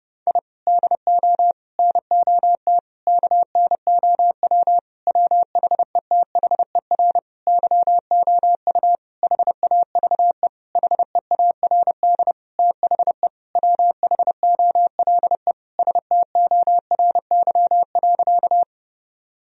Morse code, heard as a keyed tone, about 30 wpm, 700 Hz. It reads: I DO NOT KNOW WHETHER YOU HAVE HEARD THE WHOLE STORY.
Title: Sentences from Top 1000 Words - Review Part 3 - 30wpm